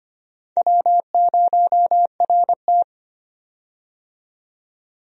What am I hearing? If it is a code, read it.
Morse code, heard as a keyed tone, about 25 wpm, 700 Hz. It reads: W0RT
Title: USA Call Signs - 4 Characters - 25wpm